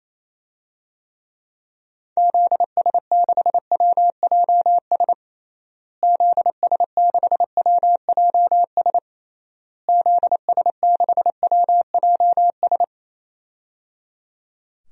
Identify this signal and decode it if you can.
Morse code, heard as a keyed tone, about 28 wpm, 700 Hz. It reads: ZS6WJS ZS6WJS ZS6WJS
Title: DXCC-100 Call Signs - 6 Characters - Repeated 3x - 28wpm